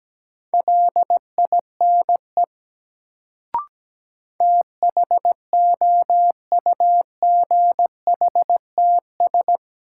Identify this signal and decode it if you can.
Morse code, heard as a keyed tone, about 17 wpm, 700 Hz. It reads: LINE THOUGHTS
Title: Sentences from Top 1000 Words - Review Part 1 - New Word Drill - 17wpm